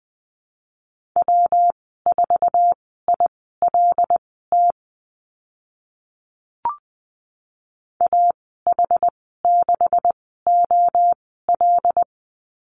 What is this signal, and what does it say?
Morse code, heard as a keyed tone, about 20 wpm, spaced as for 14 wpm, 700 Hz. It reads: W4ILT AH6OL
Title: 207 Lesson 33g - Letter j - Call Signs 20wpm